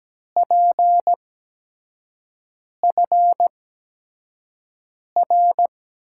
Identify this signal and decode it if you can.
Morse code, heard as a keyed tone, about 17 wpm, spaced as for 5 wpm, 700 Hz. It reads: PFR
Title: USA Call Signs - 5 Characters - 8x Character Spacing - 17wpm